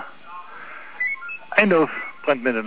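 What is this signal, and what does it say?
Five-tone selective call (ZVEI), heard as quick runs of five tones, tones of about 80 ms, with speech. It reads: 89144
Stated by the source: ZVEI five-tone selective call received on an SDR.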